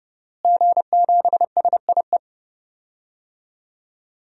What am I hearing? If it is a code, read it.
Morse code, heard as a keyed tone, about 31 wpm, 700 Hz. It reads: G7SIE
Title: DXCC-100 Call Signs - 5 Characters - Repeated 2x - 30wpm